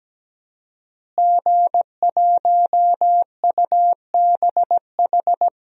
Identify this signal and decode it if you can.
Morse code, heard as a keyed tone, about 17 wpm, 700 Hz. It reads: G1UBH
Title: DXCC-100 Call Signs - 5 Characters - Repeated 3x - 17wpm